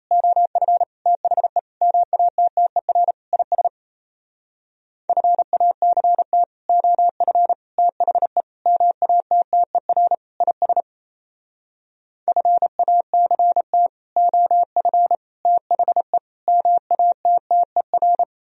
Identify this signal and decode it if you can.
Morse code, heard as a keyed tone, about 35 wpm, 700 Hz. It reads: OF THE MATTER IS FACT OF THE MATTER IS FACT OF THE MATTER
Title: Speed-Racing - Sets of 5 Words 28wpm